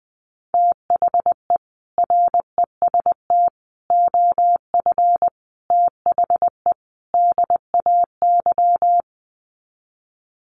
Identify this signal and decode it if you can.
Morse code, heard as a keyed tone, about 20 wpm, 700 Hz. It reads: THE REST OF THE DAY
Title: Sets of 5 Words 20wpm